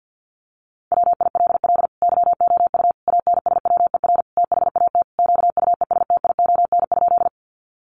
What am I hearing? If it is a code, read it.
Morse code, heard as a keyed tone, about 50 wpm, 700 Hz. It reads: WILL YOU ANSWER THAT QUESTION?
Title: Sentences from Top 400 Words 50wpm